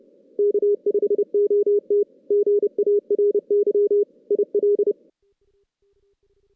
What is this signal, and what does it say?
Morse code, heard as a keyed tone, about 30 wpm, 410 Hz. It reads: K5OT GARY IL
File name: 2016 JAN CW NAQP / 2016 Jan naqp k5ot - 0-5